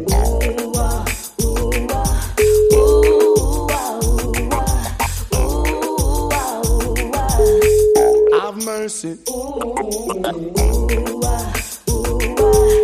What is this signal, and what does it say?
Ringback tone (Europe, 425 Hz), with noise.